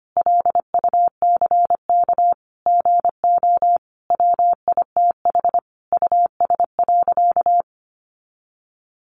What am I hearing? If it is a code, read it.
Morse code, heard as a keyed tone, about 25 wpm, 700 Hz. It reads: LUCK GO WITH US.
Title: Sentences from Top 1500 Words 25wpm